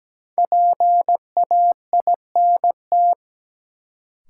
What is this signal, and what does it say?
Morse code, heard as a keyed tone, about 17 wpm, 700 Hz. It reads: PAINT
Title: Sentences from Top 1000 Words - Review Part 6 - New Word Drill - 17wpm